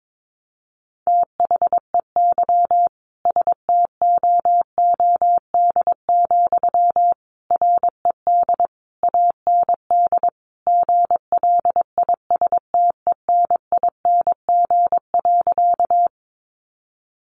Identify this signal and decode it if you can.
Morse code, heard as a keyed tone, about 22 wpm, 700 Hz. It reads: THEY STOOD, RED AND GLISTENING.
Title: A Sound Of Thunder 22wpm